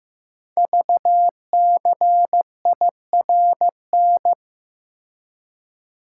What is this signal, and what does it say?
Morse code, heard as a keyed tone, about 15 wpm, 700 Hz. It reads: VCIRN
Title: Code Groups - 5 Letters - 15wpm